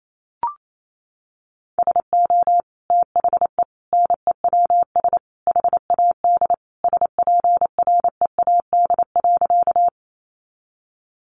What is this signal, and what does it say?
Morse code, heard as a keyed tone, about 28 wpm, 700 Hz. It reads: SO THE NEWS HAD SPREAD.
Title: Sentences from Top 1200 Words 28wpm